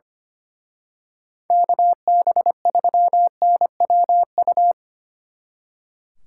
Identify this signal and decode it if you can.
Morse code, heard as a keyed tone, about 29 wpm, 700 Hz. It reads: KB3NWU
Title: Speed Racing - USA 2x3 Call Signs 25wpm